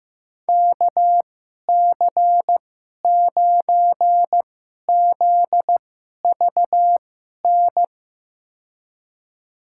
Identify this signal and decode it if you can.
Morse code, heard as a keyed tone, about 15 wpm, spaced as for 11 wpm, 700 Hz. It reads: KC9ZVN
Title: USA Call Signs - 6 Characters - 2x Character Spacing - 15wpm